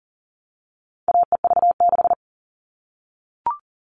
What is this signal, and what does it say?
Morse code, heard as a keyed tone, about 40 wpm, 700 Hz. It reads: AEV6
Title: All Cut Numbers - 4-Digit 40wpm